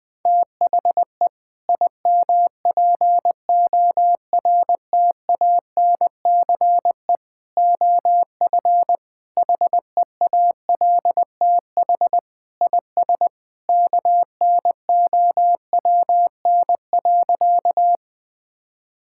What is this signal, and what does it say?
Morse code, heard as a keyed tone, about 20 wpm, 700 Hz. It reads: THE IMPORTANCE OF HEALTH IS KNOWN.
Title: Sentences with 3-Word Phrases - Spoken Beginning - 20wpm